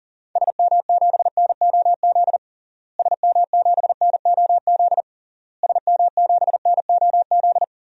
Morse code, heard as a keyed tone, about 40 wpm, 700 Hz. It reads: SM7NOZ SM7NOZ SM7NOZ
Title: DXCC-100 Call Signs - 6 Characters - Repeated 3x - 40wpm